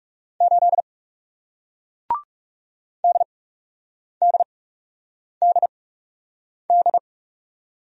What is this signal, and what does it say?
Morse code, heard as a keyed tone, about 40 wpm, spaced as for 8 wpm, 700 Hz. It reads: 8 DDDD